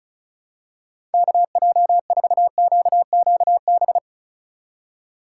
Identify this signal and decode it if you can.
Morse code, heard as a keyed tone, about 36 wpm, 700 Hz. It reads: KJ4QQB
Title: USA Call Signs - 6 Characters - 4x Character Spacing - 35wpm